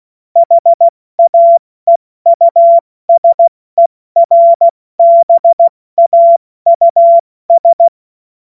Morse code, heard as a keyed tone, about 16 wpm, spaced as for 14 wpm, 680 Hz. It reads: HAEUSERBAUS
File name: Morsekurs / Witze / Witz 07.November 2023 70 F80